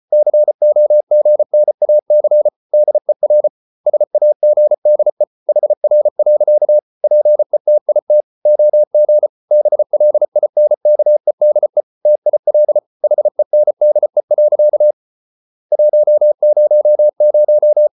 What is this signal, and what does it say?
Morse code, heard as a keyed tone, about 34 wpm, 600 Hz. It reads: COGNAC DER SAGDE HR. PETIT OG BLINKEDE TIL HENDE. 100
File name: Ferie QRQ 34wpm